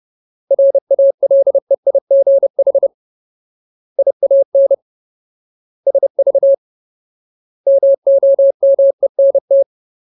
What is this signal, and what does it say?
Morse code, heard as a keyed tone, about 30 wpm, 550 Hz. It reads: RALEIGH IAN SV MOMENT
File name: Ultimate Word List / 30WPM Ultimate